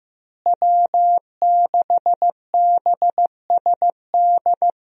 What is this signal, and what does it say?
Morse code, heard as a keyed tone, about 15 wpm, 700 Hz. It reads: W6BSD
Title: USA Call Signs - 5 Characters - Repeated 3x - 15wpm